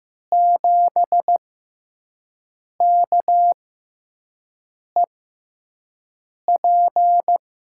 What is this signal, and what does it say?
Morse code, heard as a keyed tone, about 16 wpm, spaced as for 5 wpm, 700 Hz. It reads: 7KEP
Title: USA Call Signs - 6x Character Spacing - 15wpm